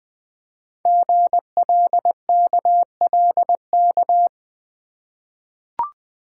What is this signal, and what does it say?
Morse code, heard as a keyed tone, about 20 wpm, 700 Hz. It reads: GLKLK
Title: Code Groups - 5 Letters - 20wpm